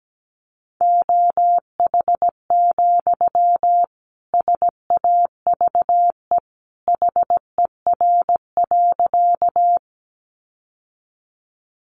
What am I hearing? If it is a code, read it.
Morse code, heard as a keyed tone, about 17 wpm, 700 Hz. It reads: OH, SAVE HER.
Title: Sentences from Top 1100 Words 17wpm